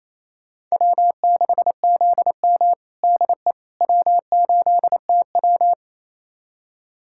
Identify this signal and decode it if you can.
Morse code, heard as a keyed tone, about 28 wpm, 700 Hz. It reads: W6ZM DE W8TW
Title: USA Callsign DE Callsign - 4 Characters - 28wpm